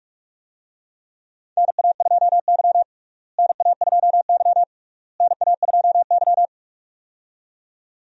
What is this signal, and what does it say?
Morse code, heard as a keyed tone, about 42 wpm, 700 Hz. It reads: NA2Y NA2Y NA2Y
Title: USA Call Signs - 4 Characters - Repeated 3x - 45wpm